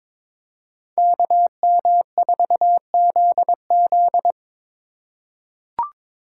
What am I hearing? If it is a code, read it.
Morse code, heard as a keyed tone, about 22 wpm, 700 Hz. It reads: KM4ZZ
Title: K1USN SST QSO Practice - 22wpm